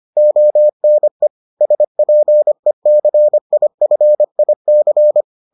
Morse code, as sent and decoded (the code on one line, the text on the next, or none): --- -. / ... .--. -.-. .. ..-. .. -.-.
ONE SPECIFIC